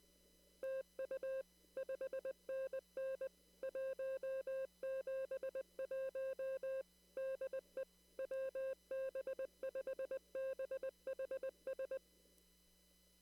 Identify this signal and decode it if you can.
Morse code, heard as a keyed tone, about 20 wpm, 530 Hz. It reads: TU 5NN 171 DE WB5BHS
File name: CW / kt5j wpx cw-op-k5pi